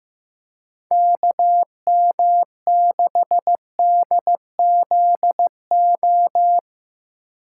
Morse code, heard as a keyed tone, about 15 wpm, 700 Hz. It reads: KM6DZO
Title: USA Call Signs - 6 Characters - 2x Character Spacing - 15wpm